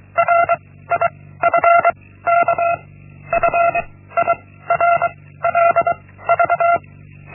Morse code, heard as a keyed tone, about 23 wpm, 680 Hz. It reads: R I F K F I R L V
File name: Linksmakalnis-1